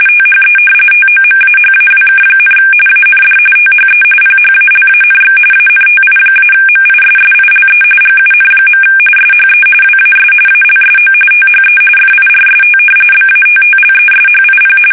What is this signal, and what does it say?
RTTY, about 45.45 baud. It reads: THE TOMAHAWK, AND THE PHALANX ANTI-MISSILE GUN. SHE WAS ALSO THE FIRST BATTLESHIP
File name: fsk850 / battleshipnj